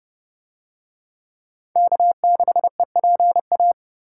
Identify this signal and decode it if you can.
Morse code, heard as a keyed tone, about 30 wpm, 700 Hz. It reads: K6EPA